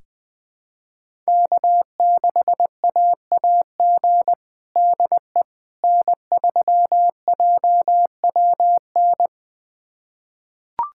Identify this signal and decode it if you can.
Morse code, heard as a keyed tone, about 21 wpm, 700 Hz. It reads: K6AAG DE N3JWN
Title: USA Callsign DE Callsign - 5 Characters - 20wpm